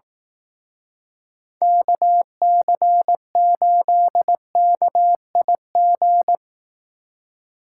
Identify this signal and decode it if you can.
Morse code, heard as a keyed tone, about 18 wpm, 700 Hz. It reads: KC8KIG
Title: Speed Racing - USA 2x3 Call Signs 15wpm